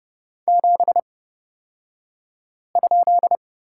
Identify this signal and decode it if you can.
Morse code, heard as a keyed tone, about 30 wpm, spaced as for 5 wpm, 700 Hz. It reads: 7?
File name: Lesson 37d - Letter x - Two Characters 30wpm